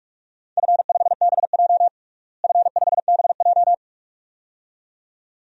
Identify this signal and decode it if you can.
Morse code, heard as a keyed tone, about 46 wpm, 700 Hz. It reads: F5BJ F5BJ
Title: DXCC-100 Call Signs - 4 Characters - Repeated 2x - 45wpm